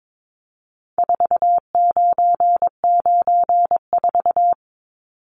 Morse code, heard as a keyed tone, about 22 wpm, 700 Hz. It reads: 4994
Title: Numbers - 4-Digit 22wpm